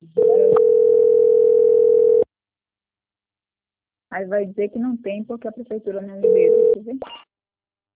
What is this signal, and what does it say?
Ringback tone (North America), 440 + 480 Hz, with speech.
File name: out- - - - - .289